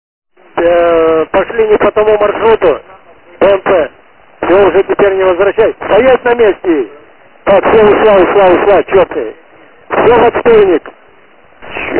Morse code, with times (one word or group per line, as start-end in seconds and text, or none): none